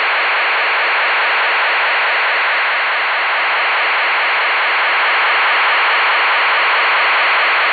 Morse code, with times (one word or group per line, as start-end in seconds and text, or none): none